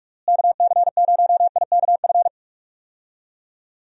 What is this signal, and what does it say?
Morse code, heard as a keyed tone, about 41 wpm, 700 Hz. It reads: KC0IKF
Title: USA Call Signs - 6 Characters - Repeated 3x - 45wpm